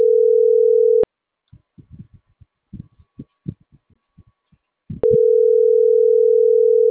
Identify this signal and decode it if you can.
Ringback tone (North America), 440 + 480 Hz.